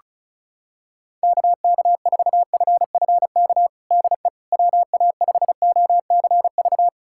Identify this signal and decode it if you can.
Morse code, heard as a keyed tone, about 36 wpm, 700 Hz. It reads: KK4FFK DE WA5OCV
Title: USA Callsign DE Callsign - 6 Characters - 35wpm